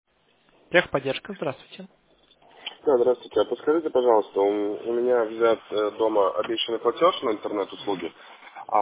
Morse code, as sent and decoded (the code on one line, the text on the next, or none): none